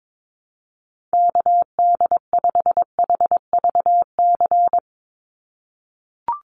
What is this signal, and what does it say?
Morse code, heard as a keyed tone, about 22 wpm, 700 Hz. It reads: KD5HVC
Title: USA Call Signs 22wpm